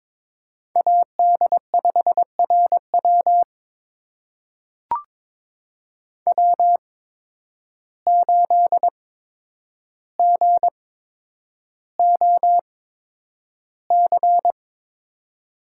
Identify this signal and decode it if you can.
Morse code, heard as a keyed tone, about 22 wpm, spaced as for 9 wpm, 700 Hz. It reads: AD5RW W8GOC
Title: USA Call Signs - 5 Characters - 8x Character Spacing - 22wpm